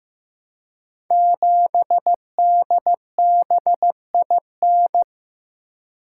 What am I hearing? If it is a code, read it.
Morse code, heard as a keyed tone, about 15 wpm, 700 Hz. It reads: 7DBIN
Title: Code Groups - 5 Letter:Numbers - 15wpm